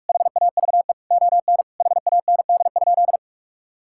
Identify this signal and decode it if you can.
Morse code, heard as a keyed tone, about 41 wpm, 700 Hz. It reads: HAVE ON HAND?
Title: Sentences from Top 1000 Words - Review Part 5 - 45wpm